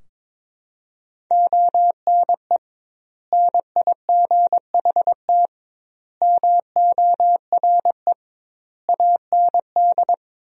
Morse code, heard as a keyed tone, about 22 wpm, 700 Hz. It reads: ONE NIGHT MORE AND
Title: 2x Word-Spacing - Sentences from Top 300 Words - 22wpm